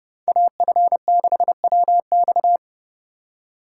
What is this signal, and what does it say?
Morse code, heard as a keyed tone, about 28 wpm, 700 Hz. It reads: AF6WX